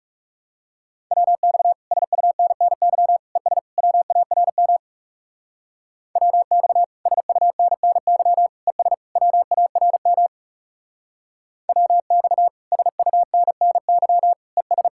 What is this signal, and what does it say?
Morse code, heard as a keyed tone, about 36 wpm, 700 Hz. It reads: WX SUNNY ES WARM WX SUNNY ES WARM WX SUNNY ES
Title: Speed-Racing Common QSO Phrases Encore - 30wpm